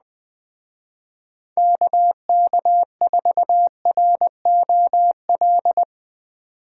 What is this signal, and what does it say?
Morse code, heard as a keyed tone, about 21 wpm, 700 Hz. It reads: KK4ROL